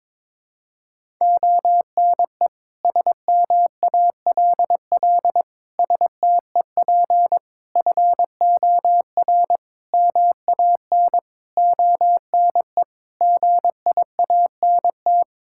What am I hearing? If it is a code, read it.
Morse code, heard as a keyed tone, about 22 wpm, 700 Hz. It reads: ONE SMALL STEP FOR MAN ONE GIANT
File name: Sets of 9 Words - 22wpm